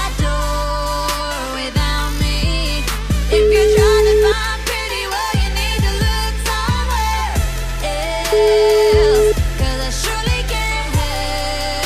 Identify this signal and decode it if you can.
Ringback tone (Europe, 425 Hz), with noise.